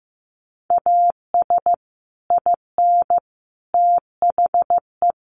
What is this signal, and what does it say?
Morse code, heard as a keyed tone, about 15 wpm, 700 Hz. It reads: AS IN THE
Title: Sets of 5 Words 15wpm